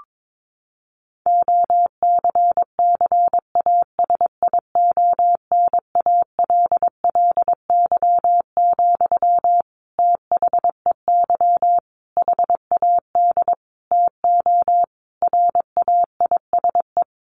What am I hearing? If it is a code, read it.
Morse code, heard as a keyed tone, about 22 wpm, 700 Hz. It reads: OCCASIONALLY, THEY HAD TO RAISE
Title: Sentences from Top 1400 Words 22wpm